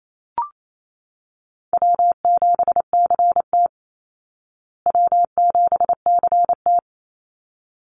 Morse code, heard as a keyed tone, about 28 wpm, 700 Hz. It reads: W7CT W7CT
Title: Easy Call Signs Repeated 2x - 28wpm